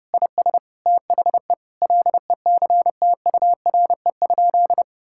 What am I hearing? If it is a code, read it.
Morse code, heard as a keyed tone, about 30 wpm, 700 Hz. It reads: IS THE LECTURE?
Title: Sentences from Top 1600 Words 30wpm